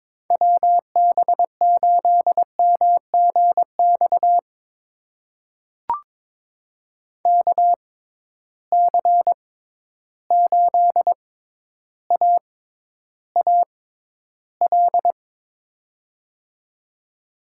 Morse code, heard as a keyed tone, about 22 wpm, spaced as for 11 wpm, 700 Hz. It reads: WB8MGX KC8AAL
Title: USA Call Signs - 6 Characters - 6x Character Spacing - 22wpm